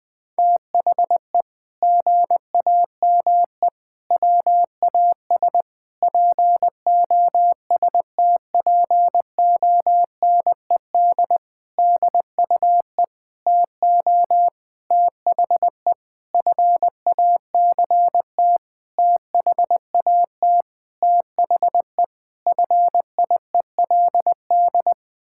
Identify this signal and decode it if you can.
Morse code, heard as a keyed tone, about 20 wpm, 700 Hz. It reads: THE GAME WAS POSTPONED DUE TO THE FACT THAT THE FIELD
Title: Sentences with 6-Word Phrases - 20wpm